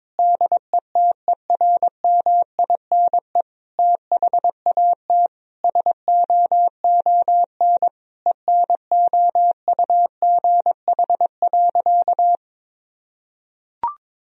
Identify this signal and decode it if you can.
Morse code, heard as a keyed tone, about 21 wpm, 700 Hz. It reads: DETERMINE THAT SOON ENOUGH.
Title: Sentences from Top 1000 Words - Review Part 4 - 22wpm